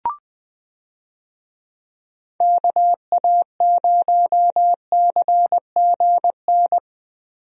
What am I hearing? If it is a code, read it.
Morse code, heard as a keyed tone, about 20 wpm, 700 Hz. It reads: KA0CGN
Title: USA Call Signs - 6 Characters - Repeated 3x - 20wpm